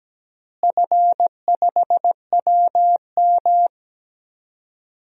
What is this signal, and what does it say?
Morse code, heard as a keyed tone, about 17 wpm, 700 Hz. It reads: F5WM
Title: DXCC-100 Call Signs - 4 Characters - Repeated 3x - 17wpm